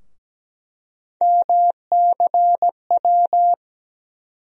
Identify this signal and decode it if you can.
Morse code, heard as a keyed tone, about 17 wpm, 700 Hz. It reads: MCW
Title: USA Call Sign Postfixes - 3 Letters 17wpm